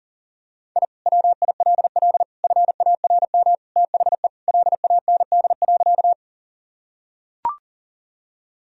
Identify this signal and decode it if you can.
Morse code, heard as a keyed tone, about 40 wpm, 700 Hz. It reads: I WILL FARM THE LAND.
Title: Sentences from Top 1000 Words - Review Part 2 - 40wpm